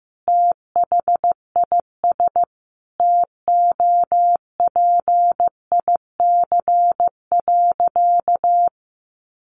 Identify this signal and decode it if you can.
Morse code, heard as a keyed tone, about 15 wpm, 700 Hz. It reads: THIS TOPIC.
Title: Sentences from Top 1200 Words 15wpm